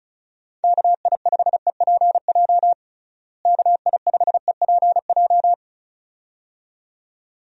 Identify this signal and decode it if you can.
Morse code, heard as a keyed tone, about 34 wpm, 700 Hz. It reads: KI5EPJ KI5EPJ
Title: USA Call Signs - 6 Characters - Repeated 2x - 35wpm